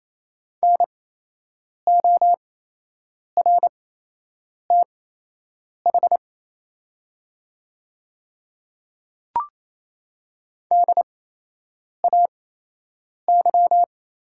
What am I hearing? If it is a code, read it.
Morse code, heard as a keyed tone, about 28 wpm, spaced as for 8 wpm, 700 Hz. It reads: NORTH DAY
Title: 8x Character Spacing - Top 300 Words - 28wpm